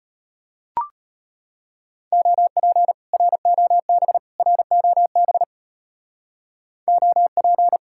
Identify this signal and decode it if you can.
Morse code, heard as a keyed tone, about 30 wpm, 700 Hz. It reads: OP ROB ROB OP
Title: Speed-Racing Common QSO Phrases - 25wpm